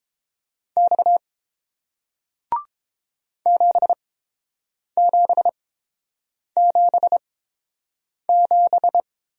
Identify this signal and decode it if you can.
Morse code, heard as a keyed tone, about 30 wpm, spaced as for 7 wpm, 700 Hz. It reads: X 7777